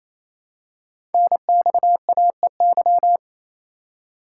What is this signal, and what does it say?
Morse code, heard as a keyed tone, about 27 wpm, 700 Hz. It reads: NXAEY
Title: Code Groups - 5 Letters - 28wpm